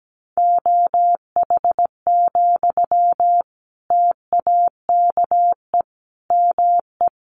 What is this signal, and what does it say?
Morse code, heard as a keyed tone, about 17 wpm, 700 Hz. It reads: OH, TAKE ME
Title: Sentences from Top 1400 Words 17wpm